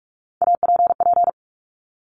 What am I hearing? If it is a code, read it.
Morse code, heard as a keyed tone, about 46 wpm, 690 Hz. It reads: APP
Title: Top Three Letter Combinations in Words 45wpm